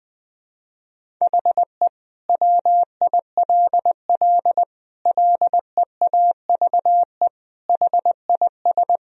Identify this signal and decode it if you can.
Morse code, heard as a keyed tone, about 20 wpm, 700 Hz. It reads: HE WILL LEAVE HIS
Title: Sentences from Top 1000 Words - Review Part 1 - 20wpm